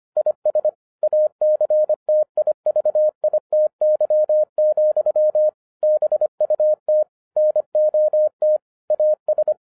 Morse code, heard as a keyed tone, about 25 wpm, 610 Hz. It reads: IS ACTIVITY, BUT NOT AS